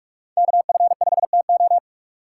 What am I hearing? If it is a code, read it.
Morse code, heard as a keyed tone, about 46 wpm, 700 Hz. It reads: KF5TO
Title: USA Call Signs - 5 Characters - 2x Character Spacing - 45wpm